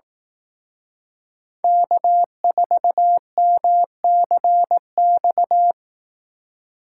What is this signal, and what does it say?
Morse code, heard as a keyed tone, about 18 wpm, 700 Hz. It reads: K4MCX
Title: Speed Racing - USA 1x3 Call Signs 15wpm